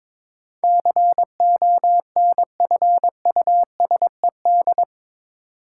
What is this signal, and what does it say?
Morse code, heard as a keyed tone, about 21 wpm, 700 Hz. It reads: CONFUSED
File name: practice / 22wpm / Eight Letter Words - 22wpm